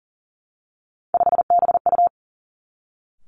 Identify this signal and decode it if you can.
Morse code, heard as a keyed tone, about 40 wpm, 700 Hz. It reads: 5BU
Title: Three Challenging Letter-Number Combinations 40wpm